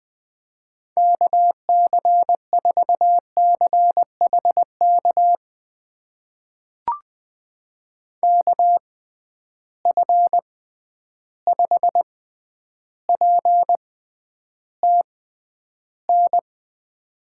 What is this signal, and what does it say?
Morse code, heard as a keyed tone, about 20 wpm, spaced as for 10 wpm, 700 Hz. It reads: KC4CHK KF5PTN
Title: USA Call Signs - 6 Characters - 6x Character Spacing - 20wpm